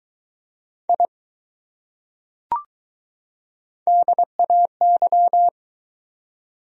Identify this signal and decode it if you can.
Morse code, heard as a keyed tone, about 21 wpm, 700 Hz. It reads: I DAY